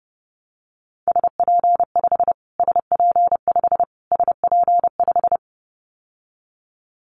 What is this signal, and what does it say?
Morse code, heard as a keyed tone, about 30 wpm, 700 Hz. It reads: SP5 SP5 SP5
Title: Callsign Prefixes for Top 25 DXCC - Repeated 3x - 30wpm